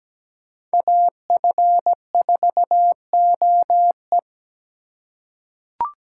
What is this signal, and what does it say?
Morse code, heard as a keyed tone, about 17 wpm, 700 Hz. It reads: AF4OE
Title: USA Call Signs - 4x Character Spacing - 17wpm